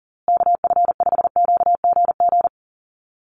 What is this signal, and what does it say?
Morse code, heard as a keyed tone, about 40 wpm, 700 Hz. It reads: KF5QGG
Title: USA Call Signs Repeated 2x 40wpm